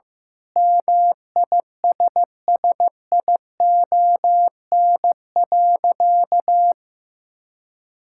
Morse code, heard as a keyed tone, about 16 wpm, 700 Hz. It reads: MISSION.
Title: Sentences from Top 1600 Words 15wpm